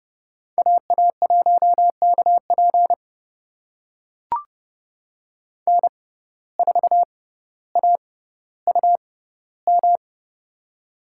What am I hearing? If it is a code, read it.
Morse code, heard as a keyed tone, about 30 wpm, spaced as for 15 wpm, 700 Hz. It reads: AA1KP N4AUM
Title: USA Call Signs - 5 Characters - 6x Character Spacing - 30wpm